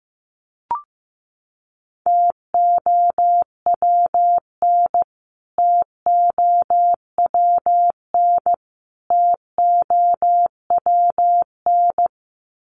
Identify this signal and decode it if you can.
Morse code, heard as a keyed tone, about 15 wpm, 700 Hz. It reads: TOWN TOWN TOWN
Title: Top 500 Words Repeated 15wpm